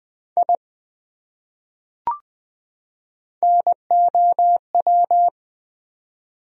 Morse code, heard as a keyed tone, about 20 wpm, 700 Hz. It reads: I NOW
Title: Top 100 Words 20wpm